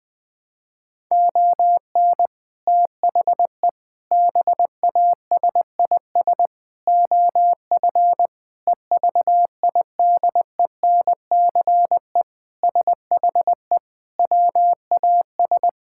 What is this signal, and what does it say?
Morse code, heard as a keyed tone, about 20 wpm, 700 Hz. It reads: ON THE BASIS OF EVIDENCE SHE WAS
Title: Sentences with 4-Word Phrases - 20wpm